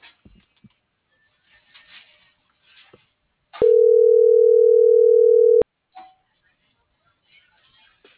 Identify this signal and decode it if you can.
Ringback tone (North America), 440 + 480 Hz.